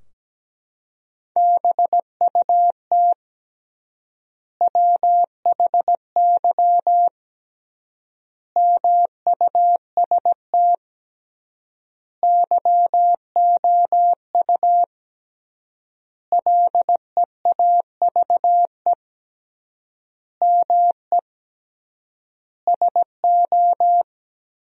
Morse code, heard as a keyed tone, about 17 wpm, 700 Hz. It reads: BUT WHY MUST YOU LEAVE ME SO
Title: 3x Word-Spacing - Sentences from Top 400 Words - 17wpm